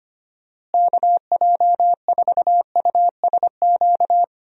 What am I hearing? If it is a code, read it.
Morse code, heard as a keyed tone, about 25 wpm, 700 Hz. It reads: KJ4USQ
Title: USA Call Signs - 6 Characters - Repeated 2x - 25wpm